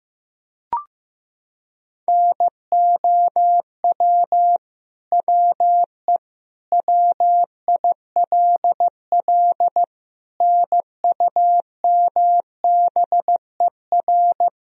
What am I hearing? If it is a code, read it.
Morse code, heard as a keyed tone, about 15 wpm, 700 Hz. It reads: NOW WE WILL NUMBER
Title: Sentences from Top 1000 Words - Review Part 1 - 15wpm